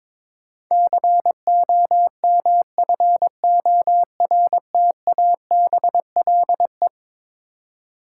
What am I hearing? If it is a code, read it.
Morse code, heard as a keyed tone, about 21 wpm, 700 Hz. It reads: COMFORTABLE
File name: practice / 22wpm / Eleven Letter Words - 22wpm